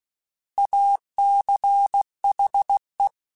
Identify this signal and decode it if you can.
Morse code, heard as a keyed tone, about 16 wpm, 800 Hz. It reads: ACHE